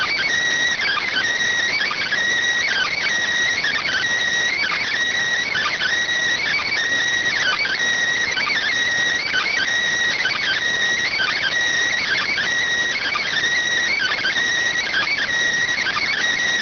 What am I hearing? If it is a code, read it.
Emergency Alert System (SAME) tones (header bursts), heard as harsh buzzing bursts. It is heard from 0.8 to 15.5 s.